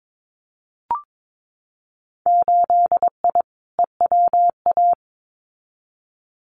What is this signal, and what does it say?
Morse code, heard as a keyed tone, about 22 wpm, 700 Hz. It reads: 8I EWA
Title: Winter Field Day Exchanges Encore 22wpm